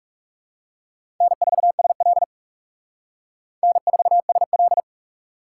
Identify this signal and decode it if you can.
Morse code, heard as a keyed tone, about 35 wpm, 700 Hz. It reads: N4SL N4SL
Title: Speed Racing - USA 1x2 Call Signs 30wpm